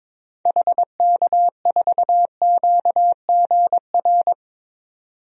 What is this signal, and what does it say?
Morse code, heard as a keyed tone, about 22 wpm, 700 Hz. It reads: HK4QGR
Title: DXCC-100 Call Signs - 6 Characters - 22wpm